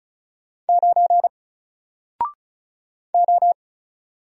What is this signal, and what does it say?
Morse code, heard as a keyed tone, about 35 wpm, 700 Hz. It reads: 9 O
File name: Single-Letter-Number-35wpm